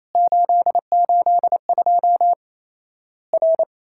Morse code, heard as a keyed tone, about 28 wpm, 700 Hz. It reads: 882 R
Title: Worked All Europe DX Contest - QTC Exchange - 28wpm